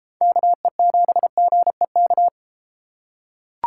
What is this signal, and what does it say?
Morse code, heard as a keyed tone, about 30 wpm, 690 Hz. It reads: KE7GEK